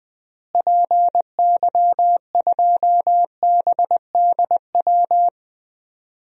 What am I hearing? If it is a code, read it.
Morse code, heard as a keyed tone, about 20 wpm, 700 Hz. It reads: PY2BDW
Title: DXCC-100 Call Signs - 6 Characters - Repeated 3x - 20wpm